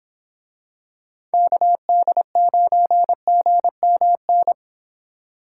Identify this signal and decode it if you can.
Morse code, heard as a keyed tone, about 26 wpm, 700 Hz. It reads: KD9GMN